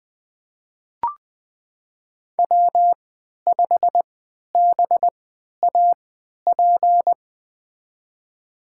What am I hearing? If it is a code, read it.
Morse code, heard as a keyed tone, about 20 wpm, spaced as for 11 wpm, 700 Hz. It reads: W5BAP
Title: USA Call Signs - 5 Characters - 3x Character Spacing - 20wpm